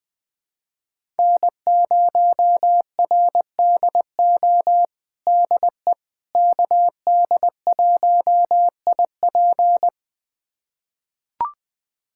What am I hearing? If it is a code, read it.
Morse code, heard as a keyed tone, about 20 wpm, 700 Hz. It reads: N0RDO DE KD1IP
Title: USA Callsign DE Callsign - 5 Characters - 20wpm